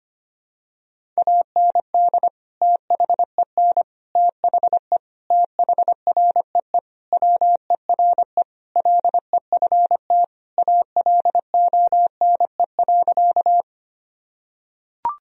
Morse code, heard as a keyed tone, about 25 wpm, 700 Hz. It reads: AND THEN THE THREE WERE LEFT ALONE.
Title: Sentences from Top 1000 Words - Review Part 1 - 25wpm